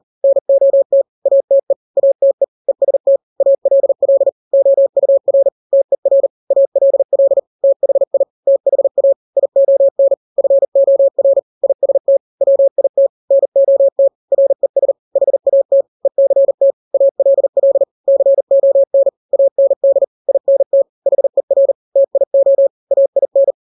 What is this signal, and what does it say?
Morse code, heard as a keyed tone, about 40 wpm, 550 Hz. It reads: NOT ATE ATE EST ALL OUR TER ALL THI THA ION FOR IST WIT NOT RES HAT ECT ALL CON AND INT HER TIO AIN
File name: n-grams / 3-grams 40 35 WPM